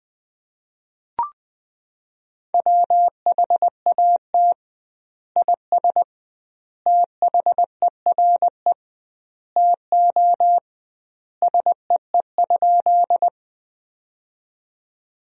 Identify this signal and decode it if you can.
Morse code, heard as a keyed tone, about 21 wpm, 700 Hz. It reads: WHAT IS THERE TO SEE?
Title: 2x Word-Spacing - Sentences from Top 100 Words - 20wpm